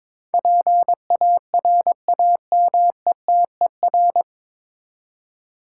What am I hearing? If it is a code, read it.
Morse code, heard as a keyed tone, about 22 wpm, 700 Hz. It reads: PARAMETER
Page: Morse Code Ninja practice 22wpm